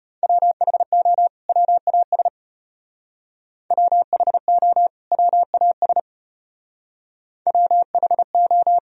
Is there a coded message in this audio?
Yes: Morse code from 0.2 to 8.8 s, heard as a keyed tone, about 34 wpm, 700 Hz. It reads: WHO WAS WHO WAS WHO